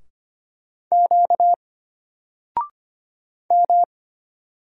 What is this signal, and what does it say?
Morse code, heard as a keyed tone, about 25 wpm, 700 Hz. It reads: Q M
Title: Single Character (Letter:Number:Punctuation:Prosign) Alt 25wpm